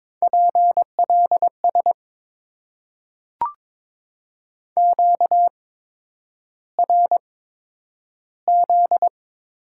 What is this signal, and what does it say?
Morse code, heard as a keyed tone, about 22 wpm, spaced as for 9 wpm, 700 Hz. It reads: PLS QRZ
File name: Top 90 QSO Elements - 8x Character Spacing - 22wpm